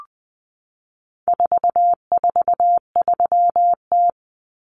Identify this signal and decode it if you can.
Morse code, heard as a keyed tone, about 20 wpm, 700 Hz. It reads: 443T
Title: Common Cut Numbers - 4-Digit 20wpm